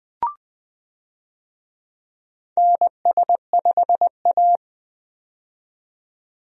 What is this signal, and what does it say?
Morse code, heard as a keyed tone, about 20 wpm, 700 Hz. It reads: NS5A
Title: USA Call Signs - 4 Characters - 20wpm